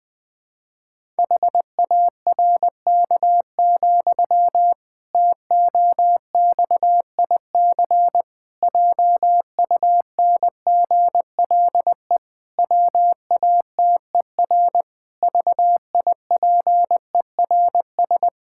Morse code, heard as a keyed tone, about 20 wpm, 700 Hz. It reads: HARK, TOXIC JUNGLE WATER VIPERS
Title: Pangram Challenge v2 - 20wpm